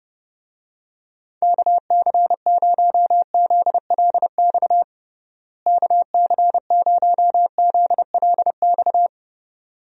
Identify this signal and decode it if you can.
Morse code, heard as a keyed tone, about 30 wpm, 700 Hz. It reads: KC0ZLX KC0ZLX
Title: USA Call Signs - 6 Characters - Repeated 2x - 30wpm